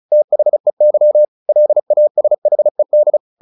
Morse code, heard as a keyed tone, about 34 wpm, 600 Hz. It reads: THEY LASHED
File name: the old man and the sea 2 35wpm